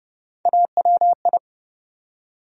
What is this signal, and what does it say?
Morse code, heard as a keyed tone, about 30 wpm, 700 Hz. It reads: AWI
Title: USA Call Sign Postfixes - 3 Letters 30wpm